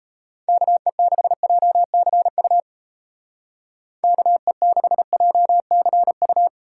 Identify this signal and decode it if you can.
Morse code, heard as a keyed tone, about 34 wpm, 700 Hz. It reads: KE6JCU KE6JCU
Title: Speed Racing - USA 2x3 Call Signs 28wpm